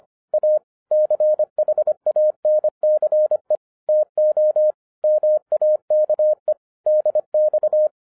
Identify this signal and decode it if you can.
Morse code, heard as a keyed tone, about 25 wpm, 610 Hz. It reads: A CHANCE TO MAKE DX